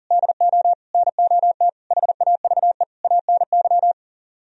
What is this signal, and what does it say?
Morse code, heard as a keyed tone, about 52 wpm, 700 Hz. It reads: DO NOT HAVE ANY